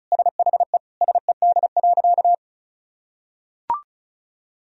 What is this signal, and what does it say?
Morse code, heard as a keyed tone, about 36 wpm, 700 Hz. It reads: SHE SED.
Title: Sentences from Top 100 Words and Abbreviations 35wpm